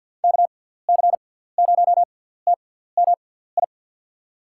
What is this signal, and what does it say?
Morse code, heard as a keyed tone, about 51 wpm, spaced as for 17 wpm, 700 Hz. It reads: KC0TMI
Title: USA Call Signs - 6 Characters - 6x Character Spacing - 50wpm